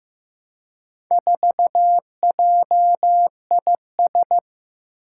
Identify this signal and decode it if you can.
Morse code, heard as a keyed tone, about 15 wpm, 700 Hz. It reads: 4JIS
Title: Four Challenging Letter-Number Combinations 15wpm